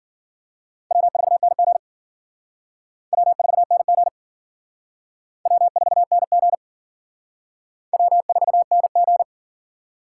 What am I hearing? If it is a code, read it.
Morse code, heard as a keyed tone, about 51 wpm, 700 Hz. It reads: W4NG W4NG W4NG W4NG